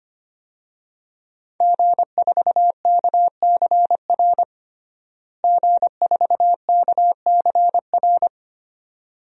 Morse code, heard as a keyed tone, about 25 wpm, 700 Hz. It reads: G4KCR G4KCR
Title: DXCC-100 Call Signs - 5 Characters - Repeated 2x - 25wpm